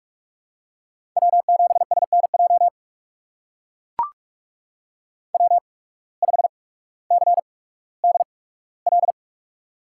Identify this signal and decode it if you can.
Morse code, heard as a keyed tone, about 44 wpm, spaced as for 19 wpm, 700 Hz. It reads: W7SNJ W5CDL
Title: USA Call Signs - 5 Characters - 8x Character Spacing - 45wpm